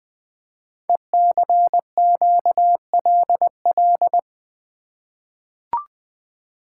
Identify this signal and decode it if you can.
Morse code, heard as a keyed tone, about 20 wpm, 700 Hz. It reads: ECQLL
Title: Code Groups - 5 Letters - 20wpm